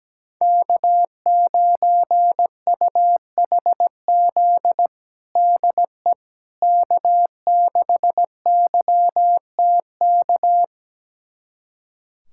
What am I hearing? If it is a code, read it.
Morse code, heard as a keyed tone, about 17 wpm, 700 Hz. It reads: K9UHZ DE K6YTK
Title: USA Callsign DE Callsign - 5 Characters - 17wpm